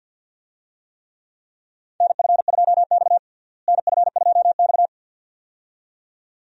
Morse code, heard as a keyed tone, about 49 wpm, 700 Hz. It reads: NF2X NF2X
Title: USA Call Signs - 4 Characters - Repeated 2x - 50wpm